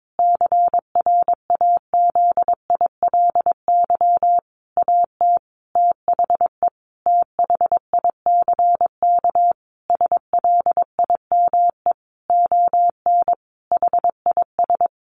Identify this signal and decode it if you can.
Morse code, heard as a keyed tone, about 22 wpm, 700 Hz. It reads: CRAZILY AT THE THICK SLIME ON HIS
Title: A Sound Of Thunder 22wpm